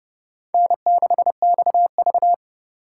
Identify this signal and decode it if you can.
Morse code, heard as a keyed tone, about 37 wpm, 700 Hz. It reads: N6XV
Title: Speed Racing - USA 1x2 Call Signs 30wpm